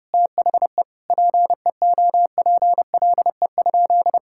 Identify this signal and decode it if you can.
Morse code, heard as a keyed tone, about 32 wpm, 700 Hz. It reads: THE PEOPLE?
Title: Sentences from Top 1000 Words - Review Part 5 - 30wpm